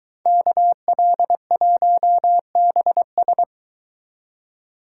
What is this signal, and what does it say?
Morse code, heard as a keyed tone, about 22 wpm, 700 Hz. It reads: KL1BS